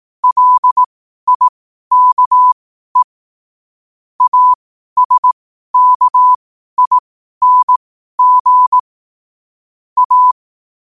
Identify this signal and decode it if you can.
Morse code, heard as a keyed tone, about 18 wpm, spaced as for 12 wpm, 1000 Hz. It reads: LIKE ASKING A
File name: qotdfiles / 12wpm